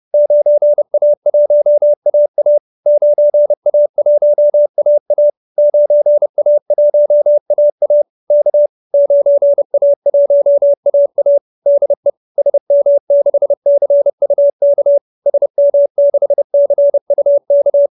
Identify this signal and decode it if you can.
Morse code, heard as a keyed tone, about 30 wpm, 590 Hz. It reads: 9A1AA 9A1AA 9A1AA K 9A1AA DE SM6CUK SM6CUK
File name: QSO319 30